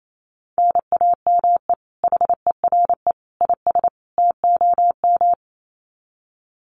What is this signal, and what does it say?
Morse code, heard as a keyed tone, about 28 wpm, 700 Hz. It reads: NAME HERE IS TOM
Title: Common QSO Phrases 28wpm